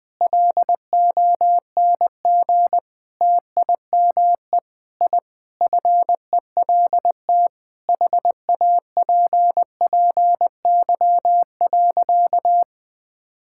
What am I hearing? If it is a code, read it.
Morse code, heard as a keyed tone, about 21 wpm, 700 Hz. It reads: LONG TIME I FELT HAPPY.
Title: Sentences with 8-Word Phrases - 20wpm